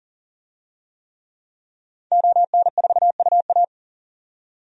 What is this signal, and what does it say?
Morse code, heard as a keyed tone, about 40 wpm, 700 Hz. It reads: ON4UA
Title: DXCC-100 Call Signs - 5 Characters - 40wpm